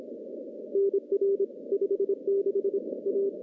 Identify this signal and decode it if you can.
Morse code, heard as a keyed tone, about 27 wpm, 410 Hz. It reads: NR 56 A
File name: contests / CW-SS-k5tr- -0200